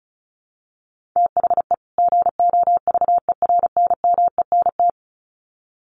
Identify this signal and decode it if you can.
Morse code, heard as a keyed tone, about 34 wpm, 700 Hz. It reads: THE GOVERNMENT
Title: Sets of 2 Words 35wpm